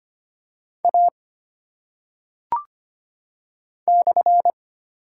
Morse code, heard as a keyed tone, about 25 wpm, 700 Hz. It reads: A /
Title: Top 180 QSO Elements 25wpm